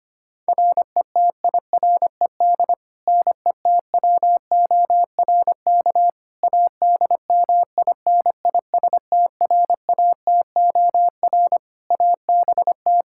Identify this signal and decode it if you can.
Morse code, heard as a keyed tone, about 25 wpm, 700 Hz. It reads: RETIRED NETWORK ADMINISTRATOR ABT
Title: Practice QSO - Exchange 3 - More Info - 25wpm